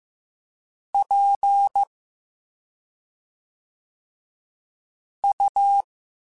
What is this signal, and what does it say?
Morse code, heard as a keyed tone, about 15 wpm, 800 Hz. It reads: P U